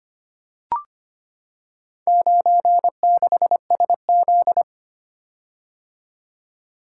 Morse code, heard as a keyed tone, about 25 wpm, 700 Hz. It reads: 96SZ
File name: practice / 25wpm / Four Letter-Number Combinations - 25wpm